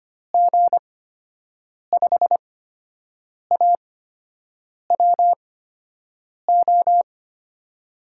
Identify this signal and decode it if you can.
Morse code, heard as a keyed tone, about 25 wpm, spaced as for 7 wpm, 700 Hz. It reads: G5AWO
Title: USA Call Signs - 6 Characters - 8x Character Spacing - 25wpm